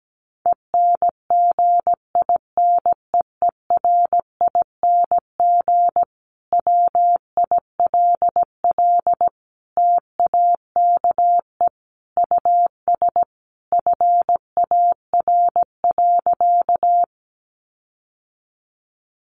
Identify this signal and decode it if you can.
Morse code, heard as a keyed tone, about 17 wpm, 700 Hz. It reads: ENGINEERING WILL TAKE US FAR.
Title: Sentences from Top 1500 Words 17wpm